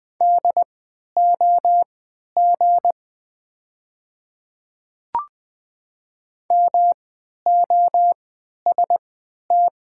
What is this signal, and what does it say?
Morse code, heard as a keyed tone, about 20 wpm, spaced as for 11 wpm, 690 Hz. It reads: DOG MOST
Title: 3x Character Spacing - Top 500 Words - 20wpm